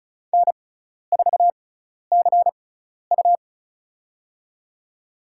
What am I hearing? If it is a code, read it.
Morse code, heard as a keyed tone, about 34 wpm, spaced as for 12 wpm, 700 Hz. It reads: N4CU
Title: USA Call Signs - 4 Characters - 6x Character Spacing - 35wpm